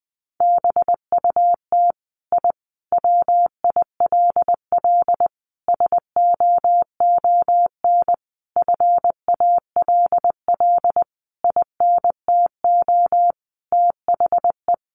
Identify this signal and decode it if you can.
Morse code, heard as a keyed tone, about 19 wpm, 700 Hz. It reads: BUT I WILL SOON FALL INTO THE
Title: Sentences from Top 1300 Words 20wpm